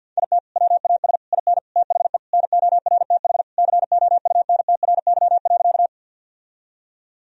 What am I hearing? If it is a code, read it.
Morse code, heard as a keyed tone, about 50 wpm, 700 Hz. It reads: IT WAS IN THE NORTH COUNTRY.
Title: Sentences from Top 1000 Words - Review Part 1 - 50wpm